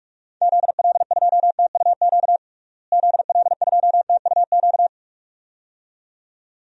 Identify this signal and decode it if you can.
Morse code, heard as a keyed tone, about 46 wpm, 700 Hz. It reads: ZL2TUQ ZL2TUQ
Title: DXCC-100 Call Signs - 6 Characters - Repeated 2x - 45wpm